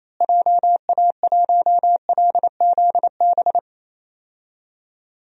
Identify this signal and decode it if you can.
Morse code, heard as a keyed tone, about 28 wpm, 700 Hz. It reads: JA1LZB
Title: DXCC-100 Call Signs - 6 Characters - Repeated 2x - 28wpm